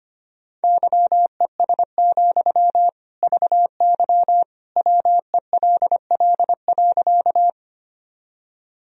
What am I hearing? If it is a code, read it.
Morse code, heard as a keyed tone, about 25 wpm, 700 Hz. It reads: YES, VY WELL.
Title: Sentences from Top 500 Words and Abbreviations 25wpm